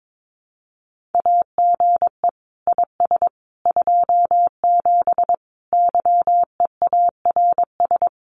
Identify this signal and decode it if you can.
Morse code, heard as a keyed tone, about 22 wpm, 700 Hz. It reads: AGE IS 27 YEARS
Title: Common QSO Phrases Encore 22wpm